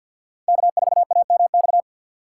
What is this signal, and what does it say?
Morse code, heard as a keyed tone, about 54 wpm, 700 Hz. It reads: K4AMX
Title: USA Call Signs - 6x Character Spacing - 50wpm